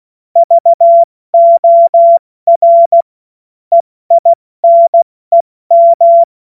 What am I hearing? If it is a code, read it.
Morse code, heard as a keyed tone, about 16 wpm, spaced as for 14 wpm, 680 Hz. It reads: VOR EINEM
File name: Morsekurs / Witze / Witz 30. Dezember 2025 70 F80